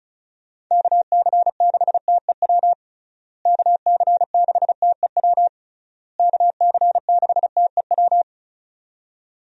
Morse code, heard as a keyed tone, about 36 wpm, 700 Hz. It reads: KC6TEW KC6TEW KC6TEW
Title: USA Call Signs - 6 Characters - Repeated 3x - 35wpm